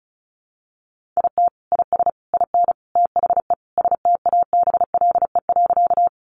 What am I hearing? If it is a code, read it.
Morse code, heard as a keyed tone, about 34 wpm, 700 Hz. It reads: IT IS IN THE STABLE.
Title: Sentences from Top 1500 Words 35wpm